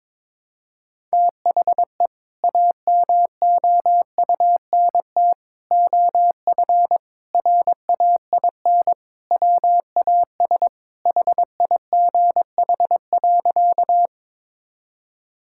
Morse code, heard as a keyed tone, about 22 wpm, 700 Hz. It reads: THE AMOUNT OF RAIN WAS HIGH.
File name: practice / 22wpm / Sentences with 3-Word Phrases - 22wpm